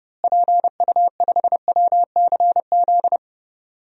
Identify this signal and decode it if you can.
Morse code, heard as a keyed tone, about 30 wpm, 700 Hz. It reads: PU5WCZ
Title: DXCC-100 Call Signs - 6 Characters - Repeated 3x - 30wpm